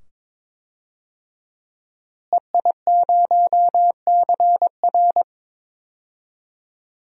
Morse code, heard as a keyed tone, about 21 wpm, 700 Hz. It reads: EI0CR
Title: DXCC-100 Call Signs - 5 Characters - Repeated 2x - 22wpm